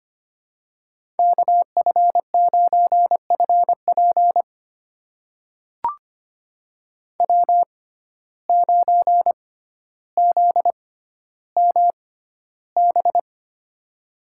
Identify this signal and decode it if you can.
Morse code, heard as a keyed tone, about 25 wpm, spaced as for 13 wpm, 700 Hz. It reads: KF9FP W9ZMB
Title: USA Call Signs - 5 Characters - 6x Character Spacing - 25wpm